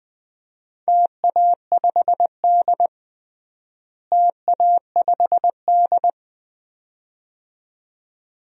Morse code, heard as a keyed tone, about 20 wpm, 700 Hz. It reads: TA5D TA5D
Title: DXCC-100 Call Signs - 4 Characters - Repeated 2x - 20wpm